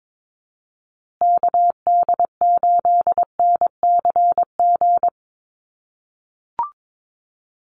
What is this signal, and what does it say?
Morse code, heard as a keyed tone, about 22 wpm, 700 Hz. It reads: KD8NCG
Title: USA Call Signs Repeated 3x 22wpm